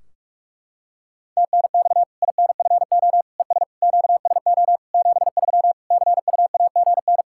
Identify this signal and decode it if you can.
Morse code, heard as a keyed tone, about 46 wpm, 700 Hz. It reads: TNX INFO ES QSO 73 CUAGN
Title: K1USN SST QSO Practice 45wpm